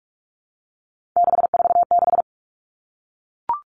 Morse code, heard as a keyed tone, about 44 wpm, 700 Hz. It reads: 646
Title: Numbers - 3-Digit 45wpm